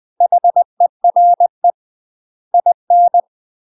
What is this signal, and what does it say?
Morse code, heard as a keyed tone, about 20 wpm, 700 Hz. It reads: HERE IN
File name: UK MASTER QSOs TRAINING / 20WPM / UK MASTER QSOs 0015